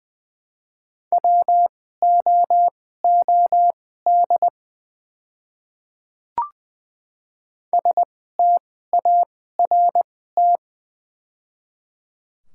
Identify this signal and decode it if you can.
Morse code, heard as a keyed tone, about 20 wpm, spaced as for 15 wpm, 700 Hz. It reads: WOOD START
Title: 2x Character Spacing - Top 300 Words - 20wpm